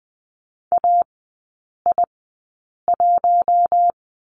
Morse code, heard as a keyed tone, about 20 wpm, spaced as for 8 wpm, 700 Hz. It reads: AI1
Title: 053 Lesson 10e - Letter r - Three Characters 20wpm